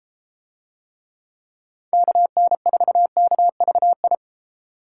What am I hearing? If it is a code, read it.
Morse code, heard as a keyed tone, about 33 wpm, 700 Hz. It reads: KN4KVI